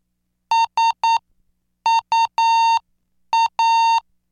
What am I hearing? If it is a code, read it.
Morse code, heard as a keyed tone, about 9 wpm, spaced as for 7 wpm, 910 Hz. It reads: SUA